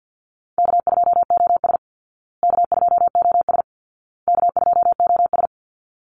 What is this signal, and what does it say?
Morse code, heard as a keyed tone, about 50 wpm, 700 Hz. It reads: K2OS K2OS K2OS
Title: Call Signs Repeated 2x - 50wpm